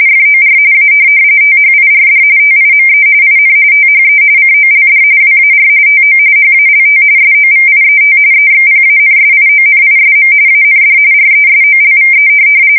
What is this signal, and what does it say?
RTTY, about 45.45 baud. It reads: FOR A THIRD TIME IN THE COLD WAR TO HALT SOVIET EXPANSIONISM AND RESOLVE
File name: fsk170 / USS-New-Jersey